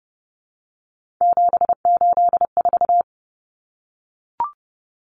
Morse code, heard as a keyed tone, about 30 wpm, 700 Hz. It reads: 784
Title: Numbers - 3-Digit 30wpm